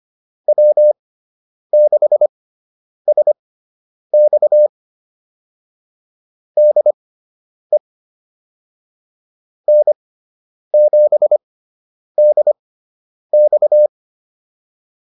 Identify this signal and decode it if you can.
Morse code, heard as a keyed tone, about 25 wpm, spaced as for 9 wpm, 600 Hz. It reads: W6SX DE N7DX